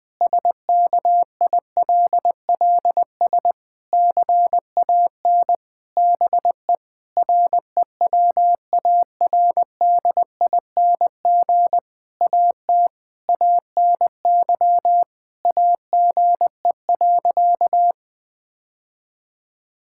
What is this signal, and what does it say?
Morse code, heard as a keyed tone, about 20 wpm, 700 Hz. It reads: SKILLS CAN BE REWARDING AT ANY AGE.
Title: Difficult Sentences with Hints - 20wpm